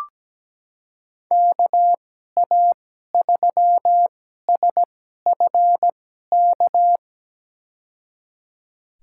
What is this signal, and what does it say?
Morse code, heard as a keyed tone, about 17 wpm, spaced as for 12 wpm, 700 Hz. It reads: KA3SFK
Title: USA Call Signs - 6 Characters - 2x Character Spacing - 17wpm